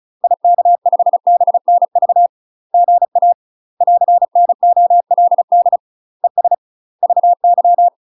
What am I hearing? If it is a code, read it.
Morse code, heard as a keyed tone, about 35 wpm, 700 Hz. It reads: IK5BNV GA <AR>NOLD ES VY
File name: UK MASTER QSOs TRAINING / 35WPM / UK MASTER QSOs 0026